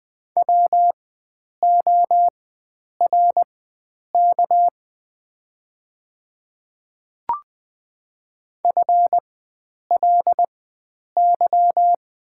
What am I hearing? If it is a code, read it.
Morse code, heard as a keyed tone, about 20 wpm, spaced as for 9 wpm, 700 Hz. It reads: WORK FLY
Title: 4x Character Spacing - Top 500 Words - 20wpm